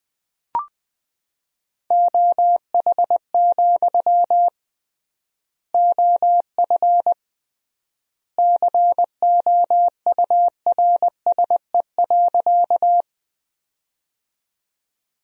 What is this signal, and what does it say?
Morse code, heard as a keyed tone, about 21 wpm, 700 Hz. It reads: OH, OF COURSE.
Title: 3x Word-Spacing - Sentences from Top 500 Words - 20wpm